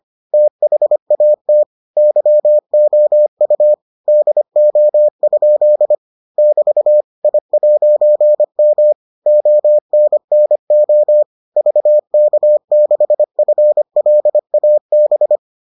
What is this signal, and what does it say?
Morse code, heard as a keyed tone, about 25 wpm, 600 Hz. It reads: THAT YOU DO? <BT> I'M ONNO VK6FLAB